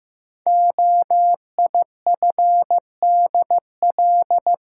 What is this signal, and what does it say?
Morse code, heard as a keyed tone, about 15 wpm, 700 Hz. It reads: OIFDL
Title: Code Groups - 5 Letters - 15wpm